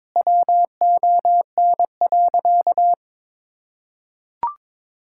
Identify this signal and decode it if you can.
Morse code, heard as a keyed tone, about 22 wpm, 700 Hz. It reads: WON.
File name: Sentences with 9-Word Phrases - 22wpm